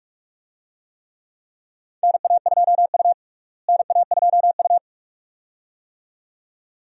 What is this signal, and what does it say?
Morse code, heard as a keyed tone, about 46 wpm, 700 Hz. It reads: NA2U NA2U
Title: USA Call Signs - 4 Characters - Repeated 2x - 45wpm